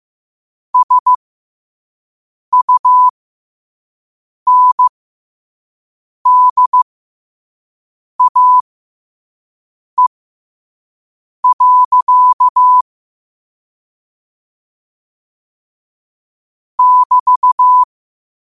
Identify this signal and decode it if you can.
Morse code, heard as a keyed tone, about 15 wpm, spaced as for 5 wpm, 1000 Hz. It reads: SUNDAE. <BT>
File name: qotdfiles / 5wpm